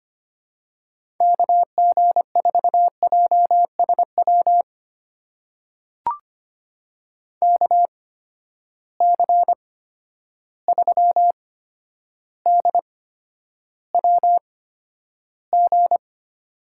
Morse code, heard as a keyed tone, about 25 wpm, spaced as for 11 wpm, 700 Hz. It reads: KG4JSW KC3DWG
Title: USA Call Signs - 6 Characters - 8x Character Spacing - 25wpm